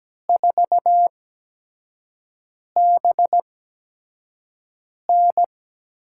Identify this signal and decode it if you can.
Morse code, heard as a keyed tone, about 17 wpm, spaced as for 5 wpm, 700 Hz. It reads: 4BN
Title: USA Call Signs - 4 Characters - 8x Character Spacing - 17wpm